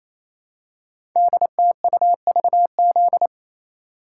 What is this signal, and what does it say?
Morse code, heard as a keyed tone, about 27 wpm, 700 Hz. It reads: DTUVZ
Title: Code Groups - 5 Letters - 28wpm